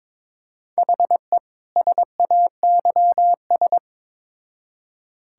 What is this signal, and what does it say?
Morse code, heard as a keyed tone, about 22 wpm, 700 Hz. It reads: HE SAYS
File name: Build A Phrase - 2-Word Phrases - 22wpm